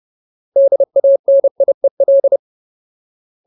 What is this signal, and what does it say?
Morse code, heard as a keyed tone, about 30 wpm, 550 Hz. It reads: DANIEL
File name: Ultimate Word List / 30WPM Ultimate